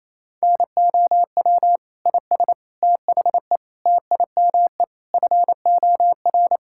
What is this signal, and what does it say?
Morse code, heard as a keyed tone, about 28 wpm, 700 Hz. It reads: NOW IS THE TIME FOR
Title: Pangram Challenge v2 - 28wpm